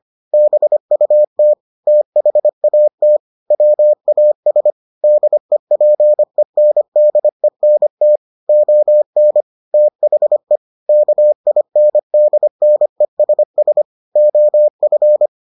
Morse code, heard as a keyed tone, about 25 wpm, 600 Hz. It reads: BUT THAT WAS DEPENDENT ON THE KINDNESS OF